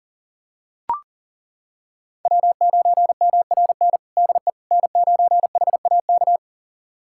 Morse code, heard as a keyed tone, about 40 wpm, 700 Hz. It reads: W9MRN DE N9HAK
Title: USA Callsign DE Callsign - 5 Characters - 40wpm